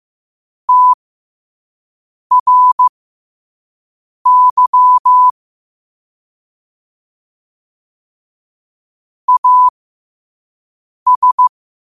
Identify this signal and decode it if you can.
Morse code, heard as a keyed tone, about 15 wpm, spaced as for 5 wpm, 1000 Hz. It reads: TRY AS
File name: qotdfiles / 5wpm